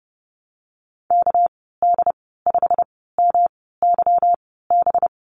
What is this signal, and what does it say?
Morse code, heard as a keyed tone, about 30 wpm, spaced as for 17 wpm, 700 Hz. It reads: KD5MYB
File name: Lesson 37g - Letter x - Call Signs 30wpm